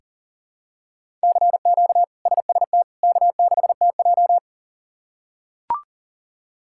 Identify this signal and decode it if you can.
Morse code, heard as a keyed tone, about 40 wpm, 700 Hz. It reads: CQ SST K6TJ
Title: K1USN SST QSO Practice - 40wpm